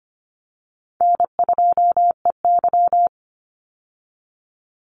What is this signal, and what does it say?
Morse code, heard as a keyed tone, about 25 wpm, 700 Hz. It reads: N2EY
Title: Call Signs Repeated 2x - 25wpm